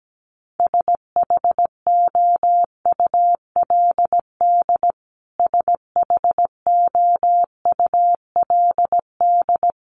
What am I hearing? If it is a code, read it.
Morse code, heard as a keyed tone, about 17 wpm, 700 Hz. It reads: SHOULD SHOULD
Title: Top 300 Words Repeated 17wpm